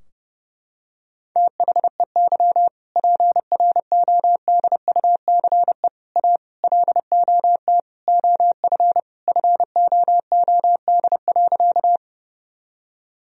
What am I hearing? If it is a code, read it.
Morse code, heard as a keyed tone, about 30 wpm, 700 Hz. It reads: THEY PRODUCE A LOT OF FOOD.
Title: Sentences from Top 1000 Words - Review Part 3 - 30wpm